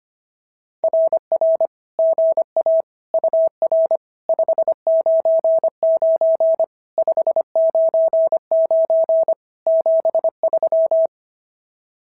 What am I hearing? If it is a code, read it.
Morse code, heard as a keyed tone, about 25 wpm, 650 Hz. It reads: RR GA UR 599 599 73
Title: SOTA QSO Practice 25wpm